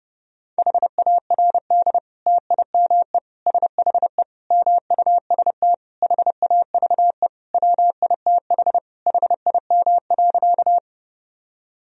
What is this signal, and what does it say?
Morse code, heard as a keyed tone, about 32 wpm, 700 Hz. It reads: HARD TIME SHE MUST HAVE WITH HIM.
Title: Sentences from Top 1000 Words - Review Part 1 - 30wpm